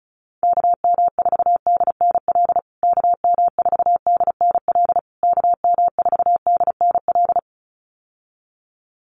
Morse code, heard as a keyed tone, about 34 wpm, 700 Hz. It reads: KM4DNL KM4DNL KM4DNL
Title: USA Call Signs Repeated 3x 35wpm